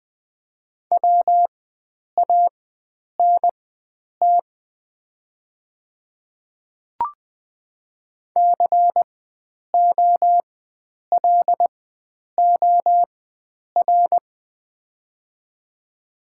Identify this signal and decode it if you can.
Morse code, heard as a keyed tone, about 20 wpm, spaced as for 9 wpm, 700 Hz. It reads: WANT COLOR
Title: 4x Character Spacing - Top 500 Words - 20wpm